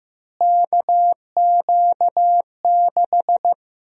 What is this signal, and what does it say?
Morse code, heard as a keyed tone, about 15 wpm, 700 Hz. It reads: KQ6
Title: USA Call Sign Prefixes - Repeated 2x - 15wpm